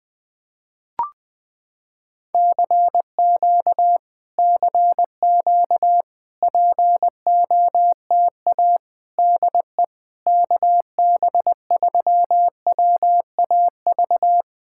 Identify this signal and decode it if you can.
Morse code, heard as a keyed tone, about 20 wpm, 700 Hz. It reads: CQ CQ POTA DE KB3WAV
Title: POTA QSO Practice - 20wpm